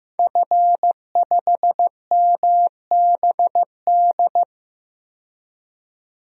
Morse code, heard as a keyed tone, about 15 wpm, 700 Hz. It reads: F5MBD
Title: DXCC-100 Call Signs - 5 Characters - 15wpm